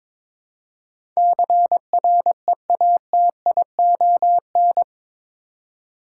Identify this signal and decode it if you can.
Morse code, heard as a keyed tone, about 22 wpm, 700 Hz. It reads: CREATION
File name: practice / 22wpm / Eight Letter Words - 22wpm